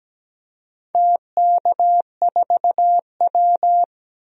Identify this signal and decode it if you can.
Morse code, heard as a keyed tone, about 17 wpm, 700 Hz. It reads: TK4W
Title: DXCC-100 Call Signs - 4 Characters - Repeated 2x - 17wpm